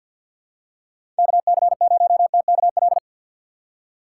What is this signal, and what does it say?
Morse code, heard as a keyed tone, about 48 wpm, 700 Hz. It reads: KC0TKL
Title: USA Call Signs - 6 Characters - 6x Character Spacing - 50wpm